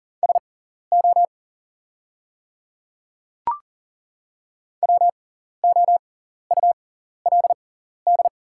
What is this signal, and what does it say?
Morse code, heard as a keyed tone, about 39 wpm, spaced as for 14 wpm, 700 Hz. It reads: SO WOULD
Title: 6x Character Spacing - Top 100 Words - 40wpm